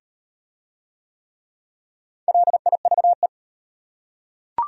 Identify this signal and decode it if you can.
Morse code, heard as a keyed tone, about 38 wpm, 700 Hz. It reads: LIVE